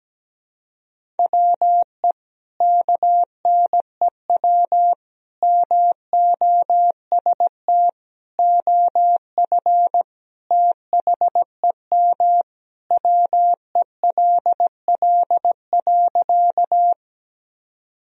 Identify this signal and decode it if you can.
Morse code, heard as a keyed tone, about 17 wpm, 700 Hz. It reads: WE KNEW MOST OF THEM WELL.
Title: Sentences from Top 1000 Words - Review Part 1 - 17wpm